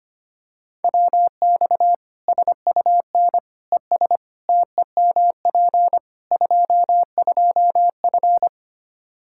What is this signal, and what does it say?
Morse code, heard as a keyed tone, about 25 wpm, 700 Hz. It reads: WX SUN ES TEMP 22F
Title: Practice QSO - Exchange 2 - Info - 25wpm